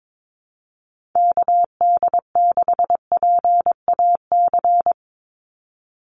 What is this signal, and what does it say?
Morse code, heard as a keyed tone, about 22 wpm, 700 Hz. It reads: KD6PAC
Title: USA Call Signs 22wpm